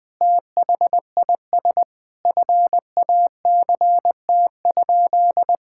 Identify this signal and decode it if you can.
Morse code, heard as a keyed tone, about 21 wpm, 700 Hz. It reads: THIS FACT?
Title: Sentences with 2-Word Phrases - 20wpm